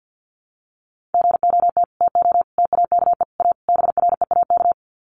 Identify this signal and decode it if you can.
Morse code, heard as a keyed tone, about 50 wpm, 700 Hz. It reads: GOT TO TAKE A BREAK
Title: Sets of 5 Words Encore 50wpm